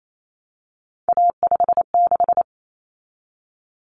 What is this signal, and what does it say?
Morse code, heard as a keyed tone, about 28 wpm, 700 Hz. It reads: A56
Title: Common Cut Numbers - 3-Digit 28wpm